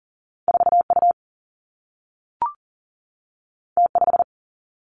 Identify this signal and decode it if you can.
Morse code, heard as a keyed tone, about 40 wpm, 700 Hz. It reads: 4U T5
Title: Two Challenging Letter-Number Combinations 40wpm